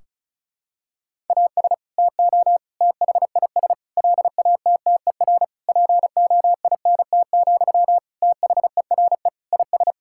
Morse code, heard as a keyed tone, about 34 wpm, 700 Hz. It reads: AS TO THIS LATTER POINT, THERE IS
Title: Sentences from Top 1600 Words 35wpm